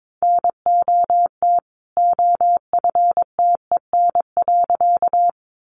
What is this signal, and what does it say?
Morse code, heard as a keyed tone, about 22 wpm, 700 Hz. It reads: NOT OFTEN.
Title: A Sound Of Thunder 22wpm